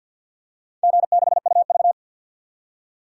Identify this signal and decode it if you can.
Morse code, heard as a keyed tone, about 51 wpm, 700 Hz. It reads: G6UV
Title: DXCC-100 Call Signs - 4 Characters - 50wpm